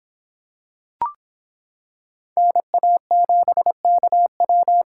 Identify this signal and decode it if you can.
Morse code, heard as a keyed tone, about 26 wpm, 700 Hz. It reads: NA7KW